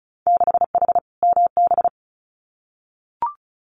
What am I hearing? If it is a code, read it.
Morse code, heard as a keyed tone, about 34 wpm, 700 Hz. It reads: 6H MB
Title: Winter Field Day Exchanges Encore 35wpm